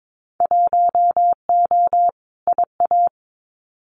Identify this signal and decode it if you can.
Morse code, heard as a keyed tone, about 22 wpm, 700 Hz. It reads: 1O IA
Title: Winter Field Day Exchanges Encore 22wpm